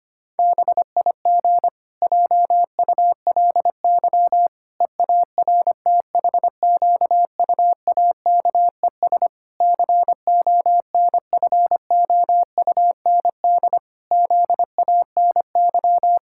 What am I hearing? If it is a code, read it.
Morse code, heard as a keyed tone, about 25 wpm, 700 Hz. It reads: BIG JULY EARTHQUAKES CONFOUND ZANY
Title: Pangram Challenge v2 - 25wpm